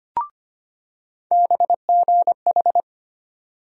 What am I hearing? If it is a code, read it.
Morse code, heard as a keyed tone, about 25 wpm, 700 Hz. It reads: BGH
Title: USA Call Sign Postfixes - 3 Letters 25wpm